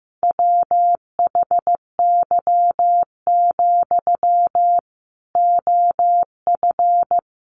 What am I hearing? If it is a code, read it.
Morse code, heard as a keyed tone, about 15 wpm, 700 Hz. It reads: WHY, OF
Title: Sentences from Top 500 Words 15wpm